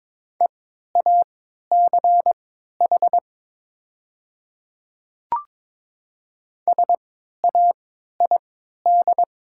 Morse code, heard as a keyed tone, about 22 wpm, spaced as for 12 wpm, 700 Hz. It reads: EACH SAID
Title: 3x Character Spacing - Top 100 Words - 22wpm